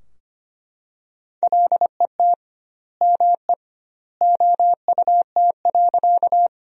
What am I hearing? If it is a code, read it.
Morse code, heard as a keyed tone, about 25 wpm, 700 Hz. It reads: LET ME OUT.
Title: 2x Word-Spacing - Sentences from Top 300 Words - 25wpm